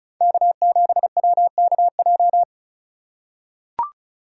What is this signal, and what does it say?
Morse code, heard as a keyed tone, about 34 wpm, 700 Hz. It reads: K7WKJ
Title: USA Call Signs - 5 Characters - 3x Character Spacing - 35wpm